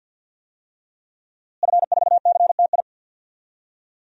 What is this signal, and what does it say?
Morse code, heard as a keyed tone, about 50 wpm, 700 Hz. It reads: F4CTI
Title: DXCC-100 Call Signs - 5 Characters - Repeated 3x - 50wpm